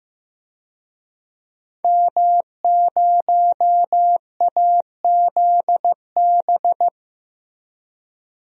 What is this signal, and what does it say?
Morse code, heard as a keyed tone, about 15 wpm, 700 Hz. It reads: M0AZB
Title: DXCC-100 Call Signs - 5 Characters - Repeated 2x - 15wpm